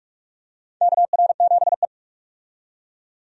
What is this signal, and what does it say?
Morse code, heard as a keyed tone, about 46 wpm, 700 Hz. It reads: KR7E
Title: USA Call Signs - 4 Characters - 45wpm